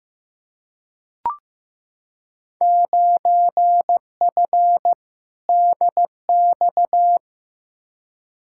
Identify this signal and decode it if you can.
Morse code, heard as a keyed tone, about 15 wpm, 700 Hz. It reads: 9F DX
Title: Field Day Exchanges Encore - 15wpm